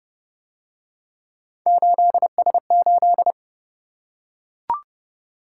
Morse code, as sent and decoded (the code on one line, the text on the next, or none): ---.. ... ---..
8S8